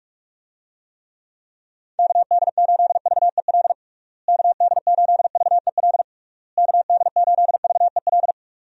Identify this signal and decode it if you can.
Morse code, heard as a keyed tone, about 41 wpm, 700 Hz. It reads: KD8VEL KD8VEL KD8VEL
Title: USA Call Signs - 6 Characters - Repeated 3x - 45wpm